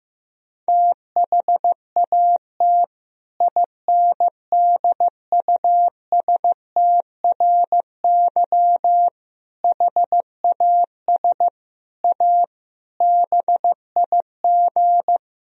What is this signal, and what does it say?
Morse code, heard as a keyed tone, about 16 wpm, 700 Hz. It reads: THAT INDUSTRY HAS A BIG
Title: Sentences from Top 1000 Words - Review Part 4 - 15wpm